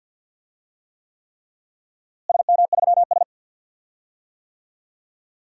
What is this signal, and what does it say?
Morse code, heard as a keyed tone, about 52 wpm, 700 Hz. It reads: SM3S
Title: DXCC-100 Call Signs - 4 Characters - Repeated 3x - 50wpm